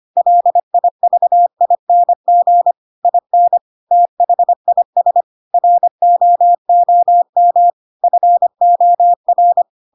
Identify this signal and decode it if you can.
Morse code, heard as a keyed tone, about 25 wpm, 700 Hz. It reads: LIVING IN THIS ROOM FOR
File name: Pinocchio / Pinocchio ch4 25 25 ik1hge